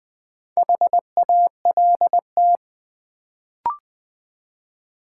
Morse code, heard as a keyed tone, about 20 wpm, 700 Hz. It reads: HALT
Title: Build A Word - 1 Letter At A Time - 4-Letter Words - 20wpm